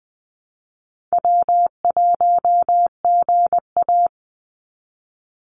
Morse code, heard as a keyed tone, about 20 wpm, 700 Hz. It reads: W1GA
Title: USA Call Signs 20wpm